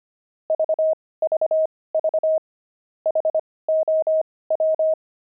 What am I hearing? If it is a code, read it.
Morse code, heard as a keyed tone, about 25 wpm, spaced as for 18 wpm, 630 Hz. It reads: VVV HOW
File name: pangram-fs-18-cs-25-daft